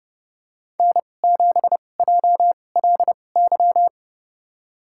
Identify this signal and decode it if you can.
Morse code, heard as a keyed tone, about 30 wpm, spaced as for 22 wpm, 700 Hz. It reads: N7JLY
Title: USA Call Signs - 2x Character Spacing - 30wpm